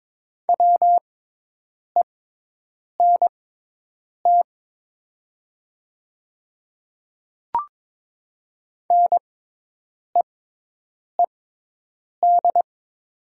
Morse code, heard as a keyed tone, about 23 wpm, spaced as for 8 wpm, 700 Hz. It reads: WENT NEED
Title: 6x Character Spacing - Top 500 Words - 22wpm